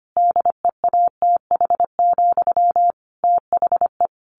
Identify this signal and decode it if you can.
Morse code, heard as a keyed tone, about 25 wpm, 700 Hz. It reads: DEATH, THE
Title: A Sound Of Thunder 25wpm